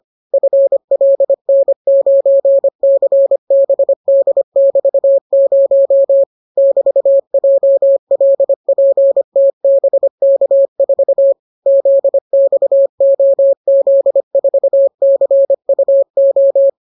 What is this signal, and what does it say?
Morse code, heard as a keyed tone, about 25 wpm, 550 Hz. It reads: FLN9CBD<BT>0 <BT>JLPTBK4 ZXOZ4CUO